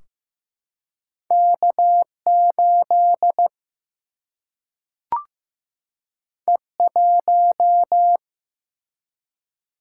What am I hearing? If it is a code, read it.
Morse code, heard as a keyed tone, about 15 wpm, 700 Hz. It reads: K 8 E 1
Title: Single-Letter Single-Number 15wpm